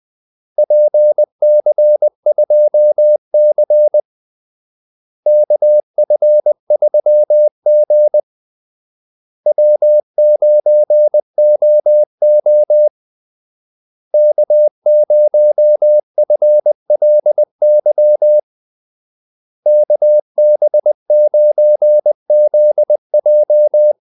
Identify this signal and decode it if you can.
Morse code, heard as a keyed tone, about 20 wpm, 600 Hz. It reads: PC2C KF3G W9OO K0FLY KB9ZJ
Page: Kaldesignaler 20 wpm